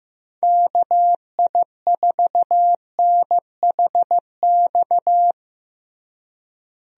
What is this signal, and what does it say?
Morse code, heard as a keyed tone, about 15 wpm, 700 Hz. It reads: KI4NHX
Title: USA Call Signs - 6 Characters - Repeated 2x - 15wpm